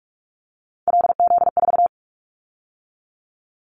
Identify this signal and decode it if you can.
Morse code, heard as a keyed tone, about 46 wpm, 700 Hz. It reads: LZ4
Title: Callsign Prefixes for Top 100 DXCC - 45wpm